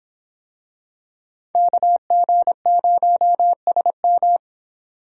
Morse code, heard as a keyed tone, about 24 wpm, 700 Hz. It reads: KG0SM